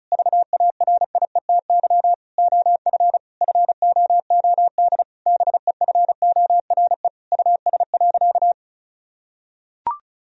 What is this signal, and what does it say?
Morse code, heard as a keyed tone, about 36 wpm, 700 Hz. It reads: VARIETY OF FOOD BEFORE US.
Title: Sentences from Top 1000 Words - Review Part 4 - 35wpm